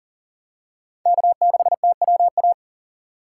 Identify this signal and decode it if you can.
Morse code, heard as a keyed tone, about 40 wpm, 700 Hz. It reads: K6TWA
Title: USA Call Signs - 5 Characters - 3x Character Spacing - 40wpm